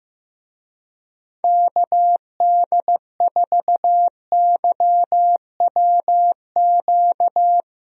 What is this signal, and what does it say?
Morse code, heard as a keyed tone, about 16 wpm, 700 Hz. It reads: KD4YWQ
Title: USA Call Signs - 6 Characters - 15wpm